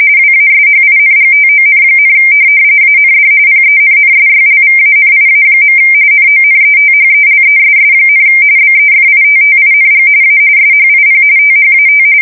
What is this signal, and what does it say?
RTTY, about 45.45 baud. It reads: WASHINGTON, AND PENNSYLVANIA. NONETHELESS, BEFITTING A SHIP THAT
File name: fsk170 / USS-New-Jersey